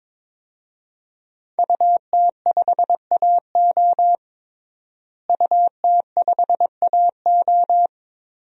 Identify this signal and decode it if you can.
Morse code, heard as a keyed tone, about 21 wpm, 700 Hz. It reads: UT5AO UT5AO
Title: DXCC-100 Call Signs - 5 Characters - Repeated 3x - 22wpm